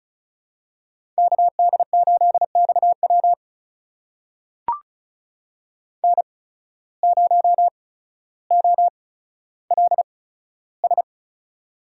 Morse code, heard as a keyed tone, about 34 wpm, spaced as for 15 wpm, 700 Hz. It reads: KD8XW N0OLS
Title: USA Call Signs - 5 Characters - 8x Character Spacing - 35wpm